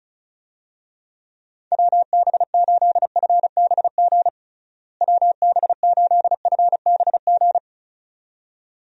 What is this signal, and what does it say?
Morse code, heard as a keyed tone, about 36 wpm, 700 Hz. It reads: WB8FBG WB8FBG
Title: USA Call Signs - 6 Characters - Repeated 2x - 35wpm